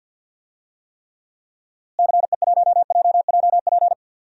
Morse code, heard as a keyed tone, about 50 wpm, 700 Hz. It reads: CE1JJP
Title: DXCC-100 Call Signs - 6 Characters - Repeated 2x - 50wpm